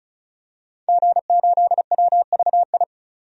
Morse code, heard as a keyed tone, about 36 wpm, 700 Hz. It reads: G8WVI
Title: DXCC-100 Call Signs - 5 Characters - 35wpm